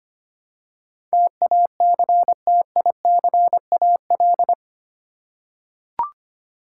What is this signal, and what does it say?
Morse code, heard as a keyed tone, about 25 wpm, 700 Hz. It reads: TACTICAL
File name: Eight Letter Words - 25wpm